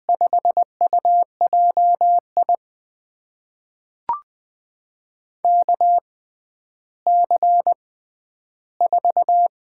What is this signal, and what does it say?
Morse code, heard as a keyed tone, about 20 wpm, spaced as for 11 wpm, 700 Hz. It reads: 5UJI KC4
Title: USA Call Signs - 6x Character Spacing - 20wpm